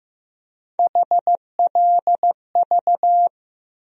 Morse code, heard as a keyed tone, about 15 wpm, 700 Hz. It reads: HLV
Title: USA Call Sign Postfixes - 3 Letters 15wpm